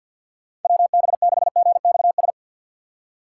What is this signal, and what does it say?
Morse code, heard as a keyed tone, about 45 wpm, 700 Hz. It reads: WB6GXS
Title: USA Call Signs - 6 Characters - 6x Character Spacing - 50wpm